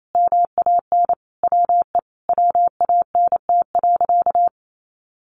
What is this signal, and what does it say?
Morse code, heard as a keyed tone, about 28 wpm, 700 Hz. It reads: MAN WE WANT.
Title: Sentences from Top 200 Words 28wpm